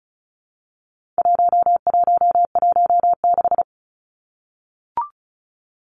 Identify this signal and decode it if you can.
Morse code, heard as a keyed tone, about 34 wpm, 700 Hz. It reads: 1116
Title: Numbers - 4-Digit 35wpm